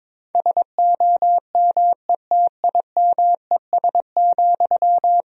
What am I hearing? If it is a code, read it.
Morse code, heard as a keyed tone, about 21 wpm, 700 Hz. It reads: SOMETIMES,
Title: Sentences from Top 1000 Words - Review Part 3 - 22wpm